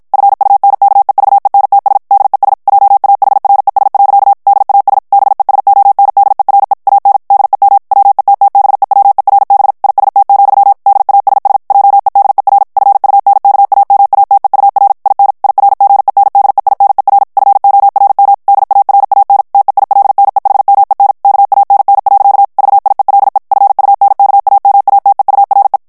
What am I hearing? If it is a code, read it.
Morse code, heard as a keyed tone, about 52 wpm, 800 Hz. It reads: FUNKEVENTS DES JAHRES. DAS BESONDERE AN DEM WETTBEWERB IST, DASS JEDER FUNKAMATEUR IN IRGENDEINER FORM DARAN TEILNEHMEN KANN. VIELE FUNKAMATEURE
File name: morse mp3 / ALLEMAND texte clair 2 vitesse 50 mots minute